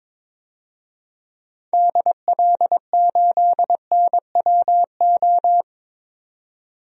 Morse code, heard as a keyed tone, about 22 wpm, 700 Hz. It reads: DL8NWO
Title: DXCC-100 Call Signs - 6 Characters - 22wpm